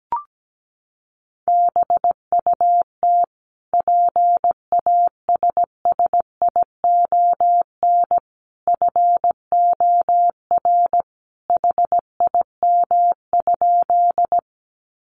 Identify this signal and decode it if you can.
Morse code, heard as a keyed tone, about 17 wpm, 700 Hz. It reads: BUT PASSION FOR HIM?
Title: Sentences from Top 1300 Words 17wpm